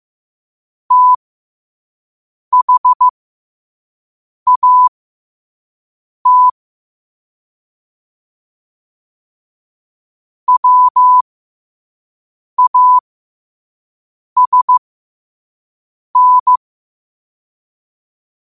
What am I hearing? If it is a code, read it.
Morse code, heard as a keyed tone, about 15 wpm, spaced as for 5 wpm, 1000 Hz. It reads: THAT WASN
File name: qotdfiles / 5wpm